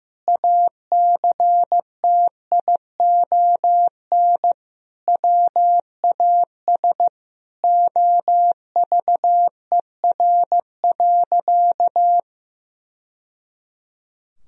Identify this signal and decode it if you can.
Morse code, heard as a keyed tone, about 15 wpm, 700 Hz. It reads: ACTION WAS OVER.
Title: Sentences from Top 1000 Words - Review Part 4 - 15wpm